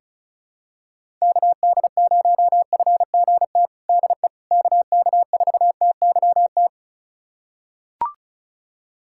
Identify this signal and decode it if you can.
Morse code, heard as a keyed tone, about 34 wpm, 700 Hz. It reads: KD0FGT DE KK4TYT
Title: USA Callsign DE Callsign - 6 Characters - 35wpm